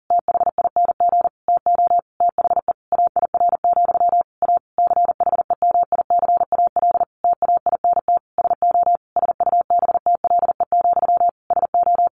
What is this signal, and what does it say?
Morse code, heard as a keyed tone, about 40 wpm, 700 Hz. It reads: THING TO THE AIR, A CHEMICAL TAINT SO SUBTLE, SO
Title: A Sound Of Thunder 40wpm